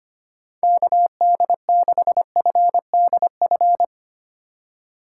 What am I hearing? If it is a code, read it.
Morse code, heard as a keyed tone, about 25 wpm, 700 Hz. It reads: KD6FDF
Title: USA Call Signs - 6 Characters - 3x Character Spacing - 25wpm